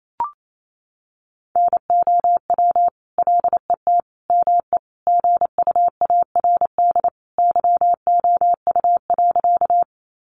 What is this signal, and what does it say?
Morse code, heard as a keyed tone, about 28 wpm, 700 Hz. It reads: NOW LET ME GUARD YOU.
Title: Sentences from Top 700 Words 28wpm